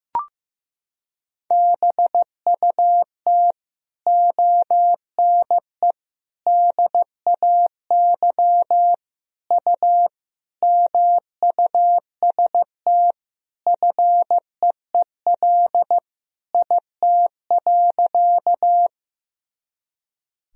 Morse code, heard as a keyed tone, about 15 wpm, 700 Hz. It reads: BUT ONE DAY U MUST FEEL IT.
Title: Sentences from Top 400 Words and Abbreviations 15wpm